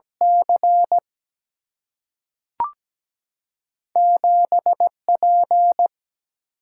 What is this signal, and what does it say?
Morse code, heard as a keyed tone, about 17 wpm, spaced as for 3 wpm, 700 Hz. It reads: C7 P
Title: Single-Letter Single-Number 17wpm